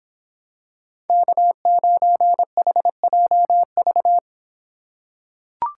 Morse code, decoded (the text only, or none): K9HJV